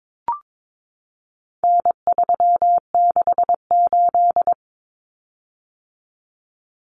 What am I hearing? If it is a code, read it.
Morse code, heard as a keyed tone, about 22 wpm, 700 Hz. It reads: N368
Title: Common Cut Numbers - 4-Digit 22wpm